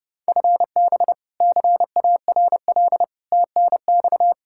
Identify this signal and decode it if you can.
Morse code, heard as a keyed tone, about 32 wpm, 700 Hz. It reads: FB CARL TNX
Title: Practice QSO - Exchange 3 - More Info - 30wpm